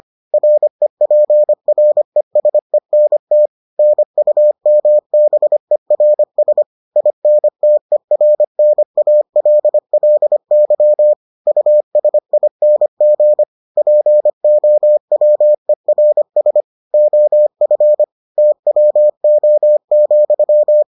Morse code, heard as a keyed tone, about 25 wpm, 600 Hz. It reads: REPRESENT NUMBERS INTERNALLY USING POWERS OF TWO,